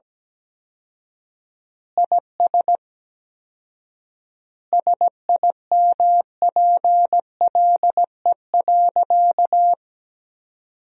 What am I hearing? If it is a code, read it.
Morse code, heard as a keyed tone, about 17 wpm, 700 Hz. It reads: IS SIMPLE.
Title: 4x Word-Spacing - Sentences from Top 400 Words - 17wpm